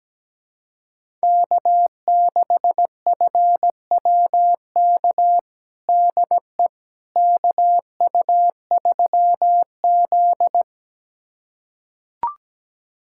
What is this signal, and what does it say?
Morse code, heard as a keyed tone, about 17 wpm, 700 Hz. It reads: K6FWK DE KU3Z
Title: USA Callsign DE Callsign 17wpm